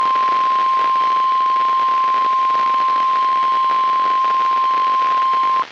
Emergency Alert System (SAME) tones (the 1050 Hz attention tone), with noise. It is heard from 0.5 to 4.8 s.
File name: eas messages / EAS WXL38 Binghamton